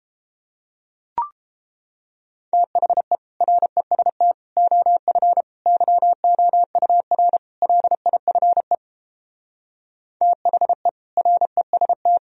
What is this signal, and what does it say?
Morse code, heard as a keyed tone, about 29 wpm, 700 Hz. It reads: THE REST OF YOUR LIFE THE REST